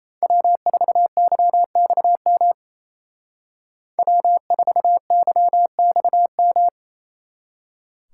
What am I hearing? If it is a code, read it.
Morse code, heard as a keyed tone, about 36 wpm, 700 Hz. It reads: W4YXM W4YXM